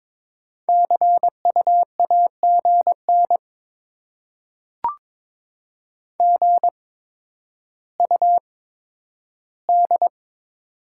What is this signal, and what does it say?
Morse code, heard as a keyed tone, about 22 wpm, spaced as for 11 wpm, 700 Hz. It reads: CU AGN GUD
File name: Top 20 QSO Elements - 8x Character Spacing - 22wpm